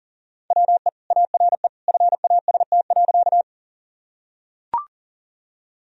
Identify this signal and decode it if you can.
Morse code, heard as a keyed tone, about 40 wpm, 700 Hz. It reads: WE ARE FAST.
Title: Sentences from Top 1000 Words - Review Part 2 - 40wpm